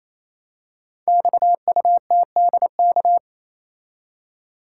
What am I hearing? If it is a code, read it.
Morse code, heard as a keyed tone, about 28 wpm, 700 Hz. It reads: XUTDK
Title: Code Groups - 5 Letters - 28wpm